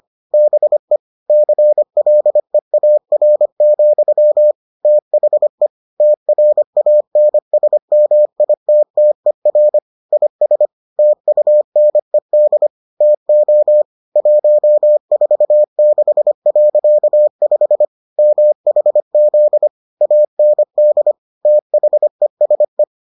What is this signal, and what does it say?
Morse code, heard as a keyed tone, about 25 wpm, 600 Hz. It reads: BE CLEAR, THE TRANSMITTER IS TUNED TO 146.5 MHZ AND THESE